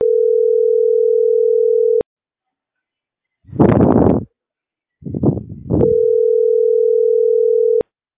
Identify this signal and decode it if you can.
Ringback tone (North America), 440 + 480 Hz.